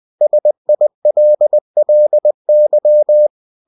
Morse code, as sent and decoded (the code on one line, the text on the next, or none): ... .. .-.. .-.. -.--
SILLY